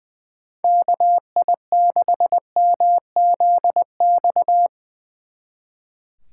Morse code, heard as a keyed tone, about 23 wpm, 700 Hz. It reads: KI6MZX
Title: Speed Racing - USA 2x3 Call Signs 20wpm